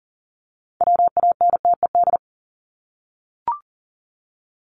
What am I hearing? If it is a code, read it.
Morse code, heard as a keyed tone, about 40 wpm, 700 Hz. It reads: WANTED
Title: ED Suffix - Top 100 Words with Base Words 40wpm